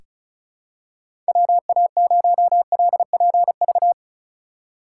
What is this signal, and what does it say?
Morse code, heard as a keyed tone, about 34 wpm, 700 Hz. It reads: WA0LPV
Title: K1USN SST QSO Practice - 35wpm